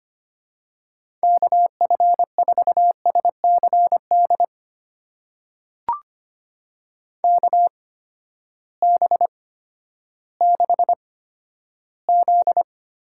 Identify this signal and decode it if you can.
Morse code, heard as a keyed tone, about 25 wpm, spaced as for 12 wpm, 700 Hz. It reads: KF4SCD KB6Z
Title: USA Call Signs - 8x Character Spacing - 25wpm